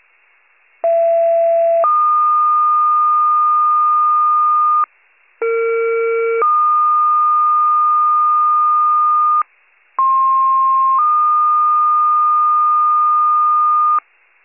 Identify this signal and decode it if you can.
Two-tone page, heard as a one-second tone, then a three-second tone, about 670 then 1220 Hz, about 460 then 1220 Hz, about 1060 then 1220 Hz. It is heard from 0.8 to 14.0 s.
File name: 98 RESCUE